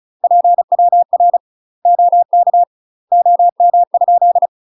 Morse code, heard as a keyed tone, about 36 wpm, 700 Hz. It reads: PWR OK OM?
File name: UK MASTER QSOs TRAINING / 35WPM / UK MASTER QSOs 0014